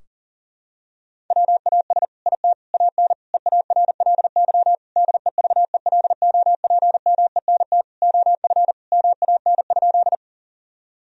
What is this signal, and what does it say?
Morse code, heard as a keyed tone, about 40 wpm, 700 Hz. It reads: WAS IT AN EARLY DEVELOPMENT OF MAN?
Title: Sentences from Top 1000 Words - Review Part 4 - 40wpm